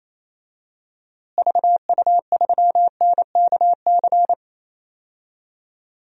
Morse code, heard as a keyed tone, about 28 wpm, 700 Hz. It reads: VU3NKC
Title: DXCC-100 Call Signs - 6 Characters - Repeated 3x - 28wpm